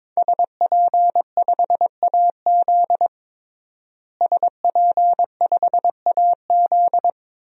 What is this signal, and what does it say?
Morse code, heard as a keyed tone, about 22 wpm, 700 Hz. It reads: SP5AZ SP5AZ
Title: DXCC-100 Call Signs - 5 Characters - Repeated 3x - 22wpm